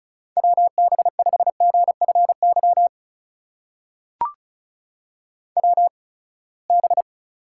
Morse code, heard as a keyed tone, about 34 wpm, spaced as for 24 wpm, 700 Hz. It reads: WB5GFY WB
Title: USA Call Signs - 6 Characters - 8x Character Spacing - 35wpm